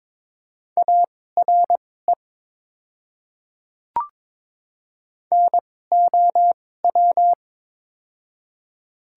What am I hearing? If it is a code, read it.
Morse code, heard as a keyed tone, about 22 wpm, spaced as for 16 wpm, 700 Hz. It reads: ARE NOW
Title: 2x Character Spacing - Top 100 Words - 22wpm